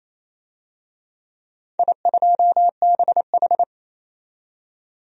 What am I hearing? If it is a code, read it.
Morse code, heard as a keyed tone, about 28 wpm, 700 Hz. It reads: I2BH
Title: DXCC-100 Call Signs - 4 Characters - 28wpm